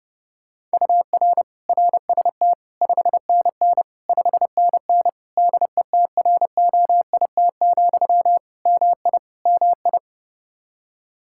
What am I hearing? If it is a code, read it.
Morse code, heard as a keyed tone, about 30 wpm, 700 Hz. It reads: UR RST 5NN 5NN DETROIT, MI MI
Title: Practice QSO - Exchange 1 - Beginning - 30wpm